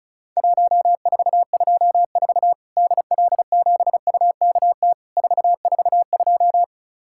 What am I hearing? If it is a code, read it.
Morse code, heard as a keyed tone, about 36 wpm, 700 Hz. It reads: 1424 DL7UKT 442
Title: Worked All Europe DX Contest - QTC Exchange - 35wpm